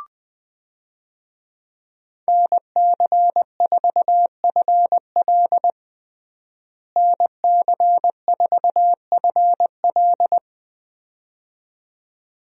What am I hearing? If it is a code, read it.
Morse code, heard as a keyed tone, about 21 wpm, 700 Hz. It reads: NC4FL NC4FL
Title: USA Call Signs - 5 Characters - Repeated 2x - 20wpm